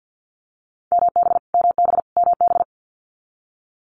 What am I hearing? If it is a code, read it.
Morse code, heard as a keyed tone, about 49 wpm, 700 Hz. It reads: MB MB MB
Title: Canadian Provinces and Territories - Repeated 3x - 50wpm